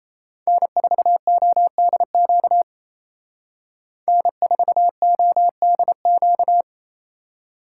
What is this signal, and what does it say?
Morse code, heard as a keyed tone, about 36 wpm, 700 Hz. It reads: N4ODQ N4ODQ